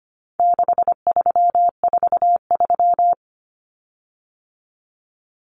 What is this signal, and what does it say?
Morse code, heard as a keyed tone, about 25 wpm, 700 Hz. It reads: 6343
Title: Common Cut Numbers - 4-Digit 25wpm